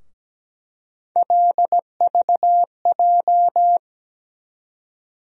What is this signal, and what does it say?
Morse code, heard as a keyed tone, about 17 wpm, 700 Hz. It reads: LVJ
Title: USA Call Sign Postfixes - 3 Letters 17wpm